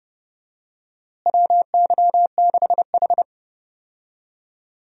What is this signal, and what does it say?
Morse code, heard as a keyed tone, about 30 wpm, 700 Hz. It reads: WY6H
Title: USA Call Signs - 4 Characters - Repeated 3x - 30wpm